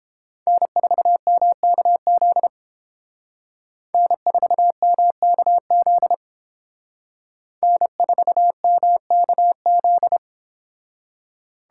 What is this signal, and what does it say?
Morse code, heard as a keyed tone, about 28 wpm, 700 Hz. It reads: N4MKZ N4MKZ N4MKZ